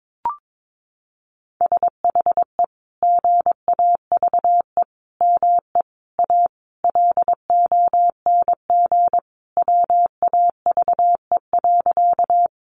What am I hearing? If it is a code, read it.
Morse code, heard as a keyed tone, about 22 wpm, 700 Hz. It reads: SHE GAVE ME A LONG WAVE.
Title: Sentences from Top 1400 Words 22wpm